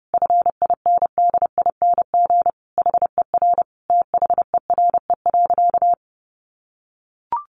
Morse code, heard as a keyed tone, about 28 wpm, 700 Hz. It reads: FINDING HER THERE.
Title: Sentences from Top 1500 Words 30wpm